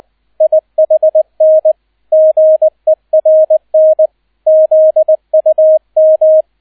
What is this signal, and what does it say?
Morse code, heard as a keyed tone, about 20 wpm, 630 Hz, with radio noise. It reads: IHN GERN ZUM